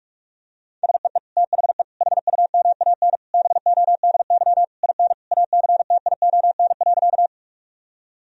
Morse code, heard as a keyed tone, about 41 wpm, 700 Hz. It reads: SEE THE HUMAN BODY IN ACTION.
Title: Sentences from Top 1000 Words - Review Part 4 - 45wpm